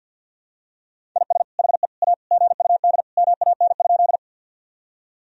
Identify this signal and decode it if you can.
Morse code, heard as a keyed tone, about 48 wpm, 700 Hz. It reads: IS HE A GUD MAN?
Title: Sentences from Top 200 Words and Abbreviations 50wpm